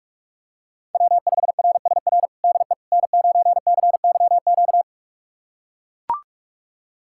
Morse code, heard as a keyed tone, about 42 wpm, 700 Hz. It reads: W5RSR DE N9CYQ
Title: USA Callsign DE Callsign - 5 Characters - 45wpm